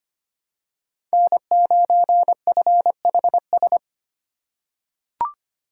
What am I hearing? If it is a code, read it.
Morse code, heard as a keyed tone, about 25 wpm, 700 Hz. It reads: N9FHS
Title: USA Call Signs - 5 Characters - 2x Character Spacing - 25wpm